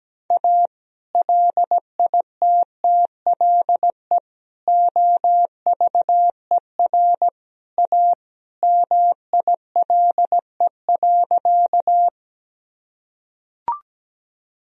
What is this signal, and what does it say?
Morse code, heard as a keyed tone, about 17 wpm, 700 Hz. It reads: A LITTLE OVER A MILE.
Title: Sentences from Top 1000 Words - Review Part 1 - 17wpm